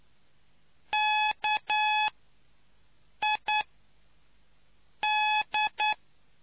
Morse code, heard as a keyed tone, about 9 wpm, spaced as for 5 wpm, 850 Hz. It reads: KID